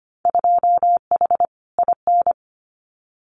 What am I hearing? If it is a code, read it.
Morse code, heard as a keyed tone, about 25 wpm, 700 Hz. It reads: 2H IN
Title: Winter Field Day Exchanges Encore 25wpm